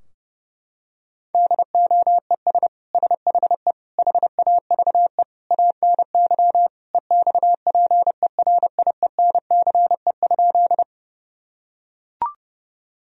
Morse code, heard as a keyed tone, about 30 wpm, 700 Hz. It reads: DOES SHE HAVE ANY EXPERIENCE?
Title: Sentences from Top 1000 Words - Review Part 3 - 30wpm